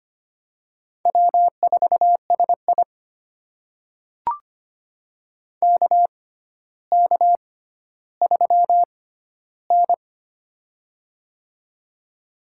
Morse code, heard as a keyed tone, about 25 wpm, spaced as for 13 wpm, 700 Hz. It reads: W4SI KK3N
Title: USA Call Signs - 4 Characters - 6x Character Spacing - 25wpm